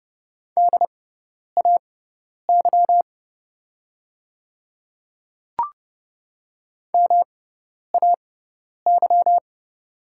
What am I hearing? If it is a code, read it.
Morse code, heard as a keyed tone, about 31 wpm, spaced as for 10 wpm, 700 Hz. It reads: DAY MAY
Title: 6x Character Spacing - Top 100 Words - 30wpm